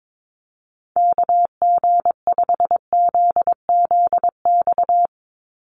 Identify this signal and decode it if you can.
Morse code, heard as a keyed tone, about 22 wpm, 700 Hz. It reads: KG5ZZX
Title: USA Call Signs 22wpm